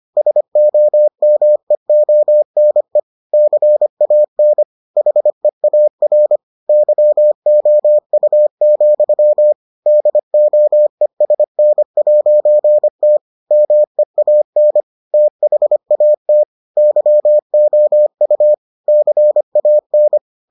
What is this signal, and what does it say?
Morse code, heard as a keyed tone, about 25 wpm, 600 Hz. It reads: SOMEONE CAN HEAR YOU, DOESN'T MEAN THAT YOU CAN